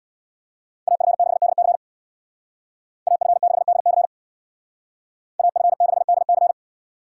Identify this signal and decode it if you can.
Morse code, heard as a keyed tone, about 64 wpm, 700 Hz. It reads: AF6DC AF6DC AF6DC